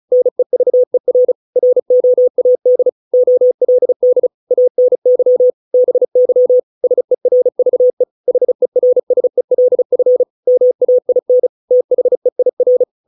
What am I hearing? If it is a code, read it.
Morse code, heard as a keyed tone, about 35 wpm, 500 Hz. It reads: NEVER ROAD OLD ANY BY SERVE HERSELF MAIN THEIR